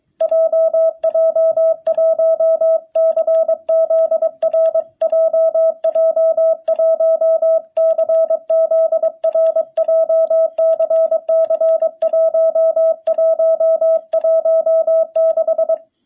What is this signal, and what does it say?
Morse code, heard as a keyed tone, about 23 wpm, 660 Hz. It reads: JJ1CZR JJ1CZR JCC1116
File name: jh8cbh / 2021cw